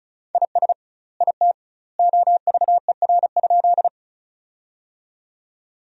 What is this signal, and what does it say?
Morse code, heard as a keyed tone, about 36 wpm, 700 Hz. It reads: IS IT OVER?
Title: 2x Word-Spacing - Sentences from Top 100 Words - 35wpm